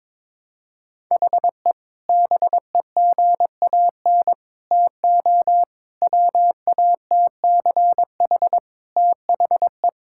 Morse code, heard as a keyed tone, about 21 wpm, 700 Hz. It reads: HE BEGAN TO WATCH THE
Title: Sentences from Top 1000 Words - Review Part 6 - 22wpm